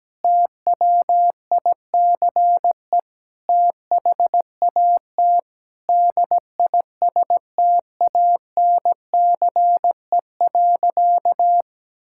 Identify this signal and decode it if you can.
Morse code, heard as a keyed tone, about 17 wpm, 700 Hz. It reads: TWICE THAT DISTANCE.
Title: Sentences from Top 1600 Words 17wpm